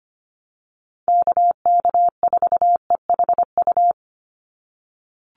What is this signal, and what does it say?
Morse code, heard as a keyed tone, about 25 wpm, 700 Hz. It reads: KK4EHU
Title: USA Call Signs 25wpm